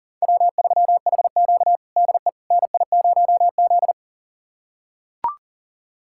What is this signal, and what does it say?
Morse code, heard as a keyed tone, about 37 wpm, 700 Hz. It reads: W3HQ DE NI0Z
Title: USA Callsign DE Callsign - 4 Characters - 40wpm